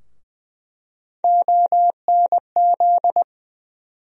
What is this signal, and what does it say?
Morse code, heard as a keyed tone, about 20 wpm, 700 Hz. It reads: ONZ
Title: USA Call Sign Postfixes - 3 Letters 20wpm